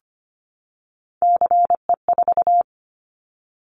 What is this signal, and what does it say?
Morse code, heard as a keyed tone, about 25 wpm, 700 Hz. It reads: CE4
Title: Callsign Prefixes for Top 100 DXCC - 25wpm